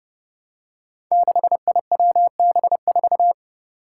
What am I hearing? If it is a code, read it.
Morse code, heard as a keyed tone, about 30 wpm, 700 Hz. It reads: 6IWB4
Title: Five Challenging Letter-Number Combinations 30wpm